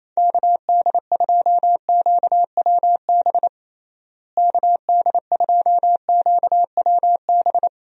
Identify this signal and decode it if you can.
Morse code, heard as a keyed tone, about 28 wpm, 700 Hz. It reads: KD2QWB KD2QWB
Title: USA Call Signs - 6 Characters - Repeated 3x - 28wpm